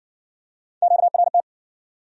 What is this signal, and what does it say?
Morse code, heard as a keyed tone, about 58 wpm, 700 Hz. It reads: QRT